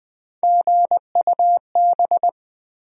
Morse code, heard as a keyed tone, about 20 wpm, 700 Hz. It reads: GUB
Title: USA Call Sign Postfixes - 3 Letters 20wpm